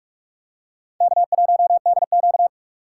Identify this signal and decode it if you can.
Morse code, heard as a keyed tone, about 46 wpm, 700 Hz. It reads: K1DQ
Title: USA Call Signs - 4 Characters - 6x Character Spacing - 45wpm